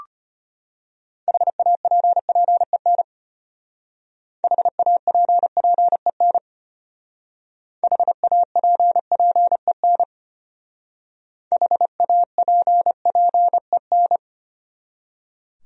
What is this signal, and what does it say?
Morse code, heard as a keyed tone, about 33 wpm, 700 Hz. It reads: HAPPEN HAPPEN HAPPEN HAPPEN